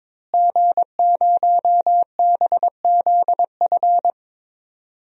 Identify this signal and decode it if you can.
Morse code, heard as a keyed tone, about 22 wpm, 700 Hz. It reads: G0BZF
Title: DXCC-100 Call Signs - 5 Characters - Repeated 2x - 22wpm